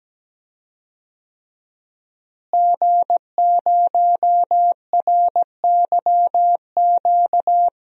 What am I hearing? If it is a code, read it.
Morse code, heard as a keyed tone, about 17 wpm, 700 Hz. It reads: G0RYQ
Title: DXCC-100 Call Signs - 5 Characters - Repeated 3x - 17wpm